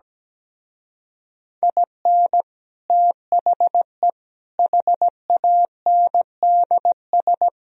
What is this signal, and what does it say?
Morse code, heard as a keyed tone, about 17 wpm, 700 Hz. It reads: IN THE HANDS
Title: Speed-Racing - Sets of 5 Words 17wpm